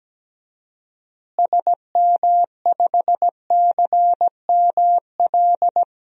Morse code, heard as a keyed tone, about 17 wpm, 700 Hz. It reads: SM5CML
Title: DXCC-100 Call Signs - 6 Characters - Repeated 3x - 17wpm